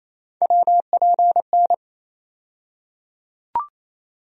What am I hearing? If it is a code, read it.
Morse code, heard as a keyed tone, about 28 wpm, 700 Hz. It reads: WPN
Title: USA Call Sign Postfixes - 3 Letters 28wpm